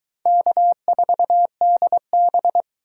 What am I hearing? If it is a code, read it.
Morse code, heard as a keyed tone, about 22 wpm, 700 Hz. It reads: K4DB